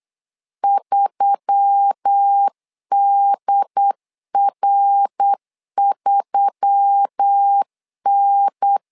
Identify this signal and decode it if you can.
Morse code, heard as a keyed tone, about 8 wpm, 800 Hz. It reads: 3DR3N
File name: LEZIONI / 4 Q1S3 8wpm 3037